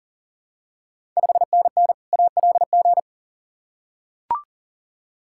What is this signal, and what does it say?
Morse code, heard as a keyed tone, about 39 wpm, 700 Hz. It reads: 5NN ALG
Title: Ontario QSO Party Exchange 40wpm